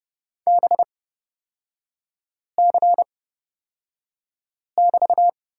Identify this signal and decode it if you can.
Morse code, heard as a keyed tone, about 30 wpm, spaced as for 5 wpm, 700 Hz. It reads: BC<BT>
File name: Lesson 36e - BT - Three Characters 30wpm